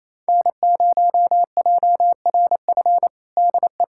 Morse code, heard as a keyed tone, about 27 wpm, 700 Hz. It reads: N0JRF DE
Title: USA Callsign DE Callsign - 5 Characters - 28wpm